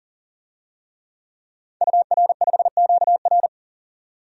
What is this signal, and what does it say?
Morse code, heard as a keyed tone, about 37 wpm, 700 Hz. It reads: UR5QR
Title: DXCC-100 Call Signs - 5 Characters - 40wpm